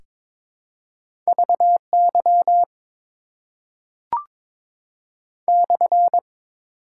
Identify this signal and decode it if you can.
Morse code, heard as a keyed tone, about 22 wpm, 700 Hz. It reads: VY /
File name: Top 90 QSO Elements - 6x Character Spacing - 22wpm